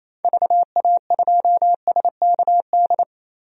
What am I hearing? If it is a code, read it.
Morse code, heard as a keyed tone, about 27 wpm, 700 Hz. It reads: VA2SKD
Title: DXCC-100 Call Signs - 6 Characters - Repeated 2x - 28wpm